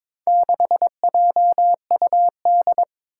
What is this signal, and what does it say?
Morse code, heard as a keyed tone, about 22 wpm, 700 Hz. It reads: 6JUD
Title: Four Challenging Letter-Number Combinations 22wpm